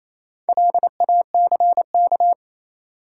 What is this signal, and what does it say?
Morse code, heard as a keyed tone, about 28 wpm, 700 Hz. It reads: LACK
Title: Sets of 3 Words - New Word Drill - 28wpm